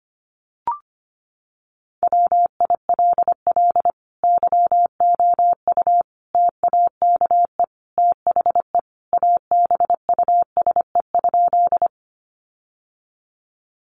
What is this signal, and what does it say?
Morse code, heard as a keyed tone, about 25 wpm, 700 Hz. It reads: WILL YOU TAKE THE ABUSE?
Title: Sentences from Top 1300 Words 25wpm